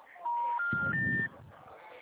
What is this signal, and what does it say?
Special information tone, with noise.
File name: out- -2010- - - .2116